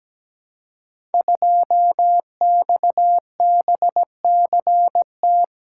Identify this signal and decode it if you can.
Morse code, heard as a keyed tone, about 17 wpm, 700 Hz. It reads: 2XBCT
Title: Code Groups - 5 Letter:Numbers - 17wpm